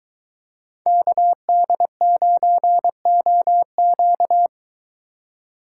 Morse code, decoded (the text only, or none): KD9OQ